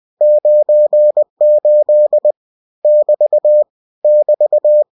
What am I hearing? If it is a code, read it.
Morse code, heard as a keyed tone, about 20 wpm, 600 Hz. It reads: 98 <BT> <BT>
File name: cw-books / winnie-the-pooh / winnie-20wpm-page-0098